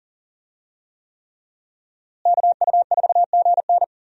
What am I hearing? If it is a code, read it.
Morse code, heard as a keyed tone, about 40 wpm, 700 Hz. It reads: KU4GN
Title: USA Call Signs - 5 Characters - Repeated 3x - 40wpm